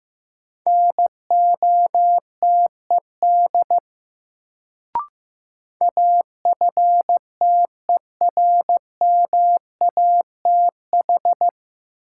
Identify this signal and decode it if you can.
Morse code, heard as a keyed tone, about 15 wpm, 700 Hz. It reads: NOTED AFTERMATH
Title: Sets of 5 Words - New Word Drill - 15wpm